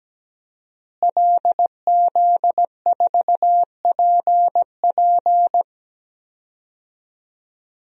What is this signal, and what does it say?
Morse code, heard as a keyed tone, about 17 wpm, 700 Hz. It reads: LZ4PP
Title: DXCC-100 Call Signs - 5 Characters - Repeated 3x - 17wpm